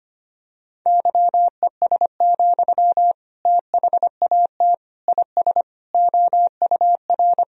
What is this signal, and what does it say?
Morse code, heard as a keyed tone, about 25 wpm, 700 Hz. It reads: YES, THAT IS OUR
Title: Sentences from Top 1000 Words - Review Part 3 - 25wpm